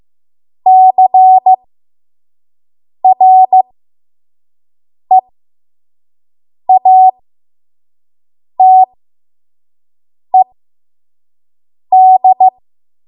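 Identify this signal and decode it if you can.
Morse code, heard as a keyed tone, about 15 wpm, spaced as for 5 wpm, 750 Hz. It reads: CREATED
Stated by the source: 5 WPM Morse code practice.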